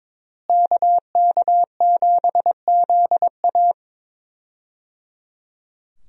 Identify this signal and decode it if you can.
Morse code, heard as a keyed tone, about 21 wpm, 700 Hz. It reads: KK7ZA
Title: USA Call Signs - 5 Characters - Repeated 3x - 22wpm